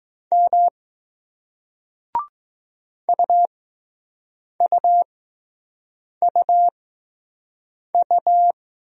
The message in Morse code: M UUUU

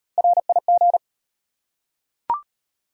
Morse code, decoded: RIG

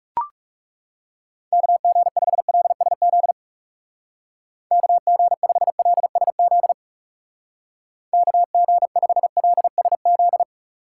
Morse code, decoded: KG5LSZ KG5LSZ KG5LSZ

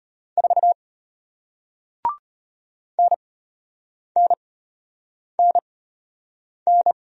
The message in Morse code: 4 NNNN

